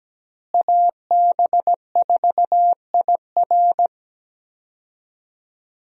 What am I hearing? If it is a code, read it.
Morse code, heard as a keyed tone, about 17 wpm, 700 Hz. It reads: AB4IR
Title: USA Call Signs - 5 Characters - Repeated 2x - 17wpm